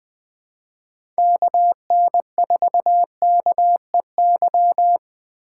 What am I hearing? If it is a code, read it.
Morse code, heard as a keyed tone, about 20 wpm, 700 Hz. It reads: KN4KEY
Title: Speed Racing - USA 2x3 Call Signs 17wpm